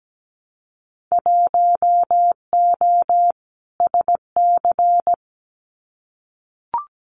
Morse code, heard as a keyed tone, about 17 wpm, 700 Hz. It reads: 1O SC
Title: Winter Field Day Exchanges 17wpm